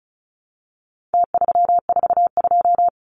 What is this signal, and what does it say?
Morse code, heard as a keyed tone, about 34 wpm, 700 Hz. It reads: T342
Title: Numbers - Contest Style - 4-Digit 35wpm